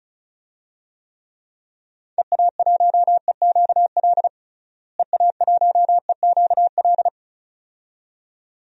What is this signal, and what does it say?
Morse code, heard as a keyed tone, about 34 wpm, 700 Hz. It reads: EA1EQL EA1EQL
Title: DXCC-100 Call Signs - 6 Characters - Repeated 2x - 35wpm